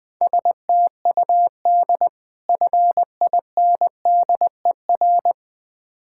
Morse code, heard as a keyed tone, about 21 wpm, 700 Hz. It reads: STUD FINDER
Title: Difficult Words with Hints - 20wpm